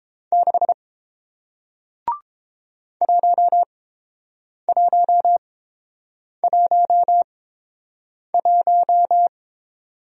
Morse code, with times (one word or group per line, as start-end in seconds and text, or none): 0.3-0.7 6
3.0-9.3 1111